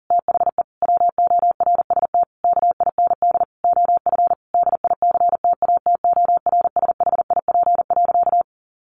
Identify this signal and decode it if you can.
Morse code, heard as a keyed tone, about 40 wpm, 700 Hz. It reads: THE WORST KIND OF DICTATORSHIP.
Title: A Sound Of Thunder 40wpm